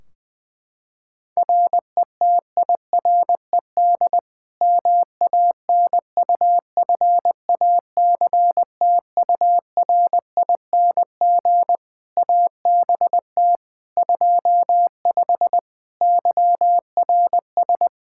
Morse code, heard as a keyed tone, about 20 wpm, 700 Hz. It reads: RETIRED MANUFACTURING ABT 25 YRS
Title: K1USN SST QSO Practice 20wpm